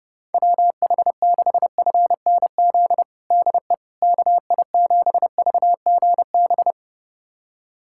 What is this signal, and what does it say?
Morse code, heard as a keyed tone, about 30 wpm, 700 Hz. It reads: WH6FNZ DE KI7VGB
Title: USA Callsign DE Callsign - 6 Characters - 30wpm